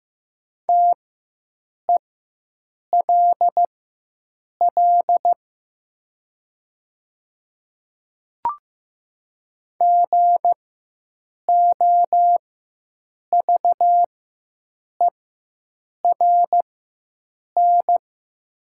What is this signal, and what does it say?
Morse code, heard as a keyed tone, about 15 wpm, spaced as for 7 wpm, 700 Hz. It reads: TELL GOVERN
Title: 4x Character Spacing - Top 500 Words - 15wpm